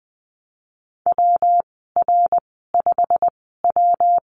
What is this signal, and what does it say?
Morse code, heard as a keyed tone, about 20 wpm, spaced as for 15 wpm, 700 Hz. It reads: WR5W
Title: 167 Lesson 27g - Number 7 - Call Signs 20wpm